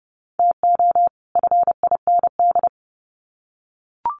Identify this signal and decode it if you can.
Morse code, heard as a keyed tone, about 30 wpm, 700 Hz. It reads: TO FIND
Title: Sets of 2 Words 30wpm